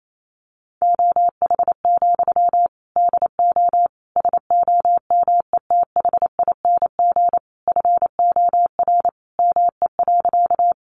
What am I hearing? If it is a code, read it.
Morse code, heard as a keyed tone, about 28 wpm, 700 Hz. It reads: OH, DO SOMETHING FOR ME.
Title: Sentences from Top 600 Words 28wpm